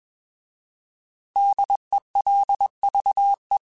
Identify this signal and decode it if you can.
Morse code, heard as a keyed tone, about 21 wpm, 800 Hz. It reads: DELVE
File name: Alphabetical / Alphabetical mp3s / Ds @20